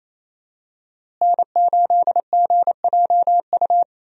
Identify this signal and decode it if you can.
Morse code, heard as a keyed tone, about 28 wpm, 700 Hz. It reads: N8GJU
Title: USA Call Signs - 5 Characters - Repeated 2x - 28wpm